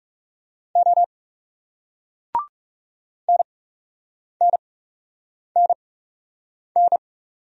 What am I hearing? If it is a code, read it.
Morse code, heard as a keyed tone, about 38 wpm, spaced as for 8 wpm, 700 Hz. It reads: O NNNN